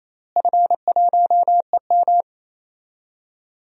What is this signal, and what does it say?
Morse code, heard as a keyed tone, about 28 wpm, 700 Hz. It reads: F1EM
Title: DXCC-100 Call Signs - 4 Characters - Repeated 3x - 28wpm